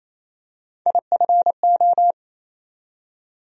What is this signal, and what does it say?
Morse code, heard as a keyed tone, about 27 wpm, 700 Hz. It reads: IFO
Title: USA Call Sign Postfixes - 3 Letters 28wpm